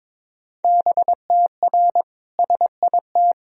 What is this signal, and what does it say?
Morse code, heard as a keyed tone, about 21 wpm, 700 Hz. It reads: BTR SIT
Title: Sentences from Top 500 Words and Abbreviations 22wpm